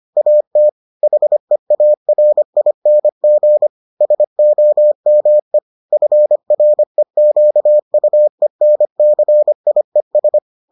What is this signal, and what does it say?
Morse code, heard as a keyed tone, about 25 wpm, 600 Hz. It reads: AT HEARING SOME FREQUENCIES